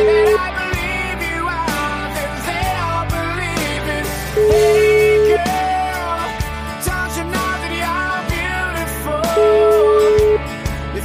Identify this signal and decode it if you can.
Ringback tone (Europe, 425 Hz), with noise.